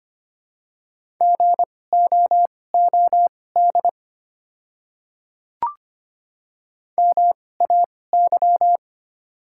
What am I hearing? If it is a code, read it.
Morse code, heard as a keyed tone, about 25 wpm, spaced as for 18 wpm, 700 Hz. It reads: GOOD MAY
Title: 2x Character Spacing - Top 200 Words - 25wpm